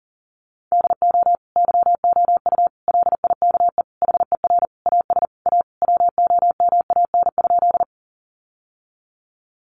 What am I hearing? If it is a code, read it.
Morse code, heard as a keyed tone, about 40 wpm, 700 Hz. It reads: DO YOU LIKE HER AS A WOMAN?
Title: Sentences from Top 1100 Words 40wpm